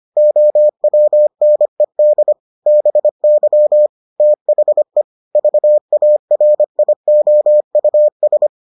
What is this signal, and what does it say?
Morse code, heard as a keyed tone, about 25 wpm, 600 Hz. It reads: OWNED BY THE VARIOUS